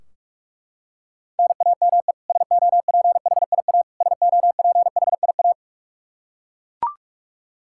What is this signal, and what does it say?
Morse code, heard as a keyed tone, about 41 wpm, 700 Hz. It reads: NAME SOPHIA SOPHIA
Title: Practice QSO - Exchange 1 - Beginning - 45wpm